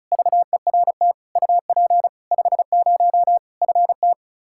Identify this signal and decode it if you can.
Morse code, heard as a keyed tone, about 34 wpm, 700 Hz. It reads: VERT UP 50 FT
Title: Practice QSO - Exchange 2 - Info - 35wpm